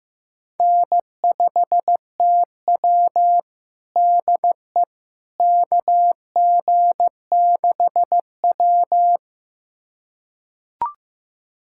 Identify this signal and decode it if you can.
Morse code, heard as a keyed tone, about 15 wpm, 700 Hz. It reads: N5TW DE KG6W
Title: USA Callsign DE Callsign - 4 Characters - 15wpm